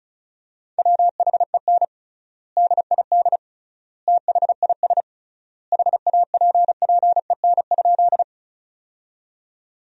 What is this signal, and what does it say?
Morse code, heard as a keyed tone, about 34 wpm, 700 Hz. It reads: WHEN DID THIS HAPPEN?
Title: 3x Word-Spacing - Sentences from Top 400 Words - 35wpm